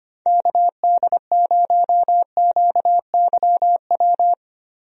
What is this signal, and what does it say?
Morse code, heard as a keyed tone, about 25 wpm, 700 Hz. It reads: KD0QYW
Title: USA Call Signs - 6 Characters - Repeated 2x - 25wpm